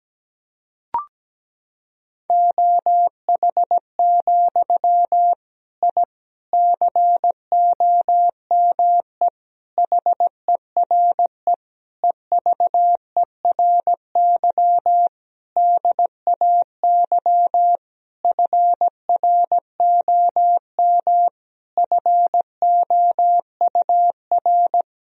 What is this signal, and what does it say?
Morse code, heard as a keyed tone, about 17 wpm, 700 Hz. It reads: OH, I COME HERE EVERY DAY FROM FOUR
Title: Sentences from Top 1000 Words - Review Part 2 - 17wpm